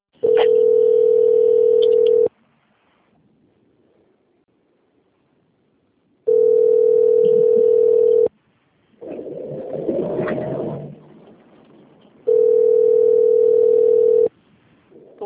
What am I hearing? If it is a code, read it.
Ringback tone (North America), 440 + 480 Hz.